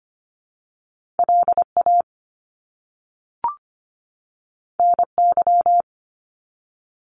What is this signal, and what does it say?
Morse code, heard as a keyed tone, about 25 wpm, 700 Hz. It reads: LA NY
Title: US States 25wpm